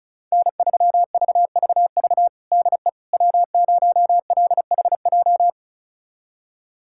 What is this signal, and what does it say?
Morse code, heard as a keyed tone, about 36 wpm, 700 Hz. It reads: N3VVV DE W0LHJ
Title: USA Callsign DE Callsign - 5 Characters - 35wpm